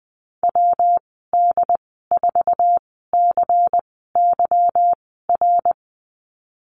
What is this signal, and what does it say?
Morse code, heard as a keyed tone, about 20 wpm, spaced as for 15 wpm, 700 Hz. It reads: WD4CYR
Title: 173 Lesson 28g - Number 9 - Call Signs 20wpm